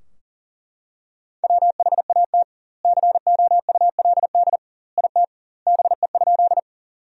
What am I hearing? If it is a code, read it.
Morse code, heard as a keyed tone, about 40 wpm, 700 Hz. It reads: WHAT COULD IT BE?
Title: 2x Word-Spacing - Sentences from Top 100 Words - 40wpm